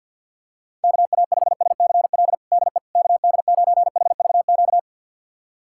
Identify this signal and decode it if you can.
Morse code, heard as a keyed tone, about 47 wpm, 700 Hz. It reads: KA5SCL DE KD9HVQ
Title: USA Callsign DE Callsign - 6 Characters - 50wpm